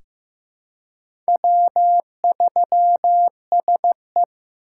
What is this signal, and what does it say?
Morse code, heard as a keyed tone, about 15 wpm, 700 Hz. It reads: W3SE
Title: USA Call Signs - 4 Characters - 3x Character Spacing - 15wpm